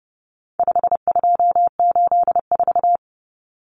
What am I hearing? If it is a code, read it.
Morse code, heard as a keyed tone, about 30 wpm, 700 Hz. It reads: 5284
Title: Common Cut Numbers - 4-Digit 30wpm